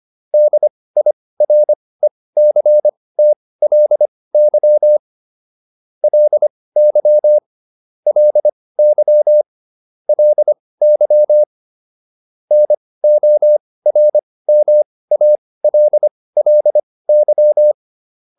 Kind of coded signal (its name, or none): Morse code